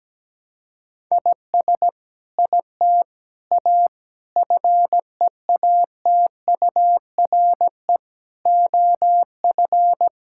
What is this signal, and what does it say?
Morse code, heard as a keyed tone, about 17 wpm, 700 Hz. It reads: IS IT A FEATURE OF
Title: Sentences from Top 1000 Words - Review Part 5 - 17wpm